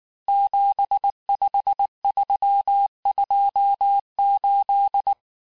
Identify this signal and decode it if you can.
Morse code, heard as a keyed tone, about 19 wpm, 800 Hz. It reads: 75328
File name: cours-du-lundi files / cours-1595-teste-CHIFFRE-vitesse18-MPM